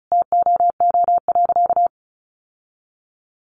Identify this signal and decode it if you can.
Morse code, heard as a keyed tone, about 34 wpm, 700 Hz. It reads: TOO.
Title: Sentences from Top 700 Words 35wpm